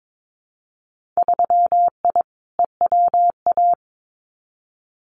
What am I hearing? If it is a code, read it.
Morse code, heard as a keyed tone, about 22 wpm, 700 Hz. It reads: 3I EWA
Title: Winter Field Day Exchanges Encore 22wpm